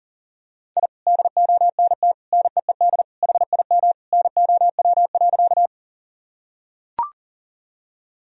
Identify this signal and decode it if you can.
Morse code, heard as a keyed tone, about 37 wpm, 700 Hz. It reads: I DONT NEED HIM NOW.
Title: Sentences from Top 1000 Words - Review Part 1 - 40wpm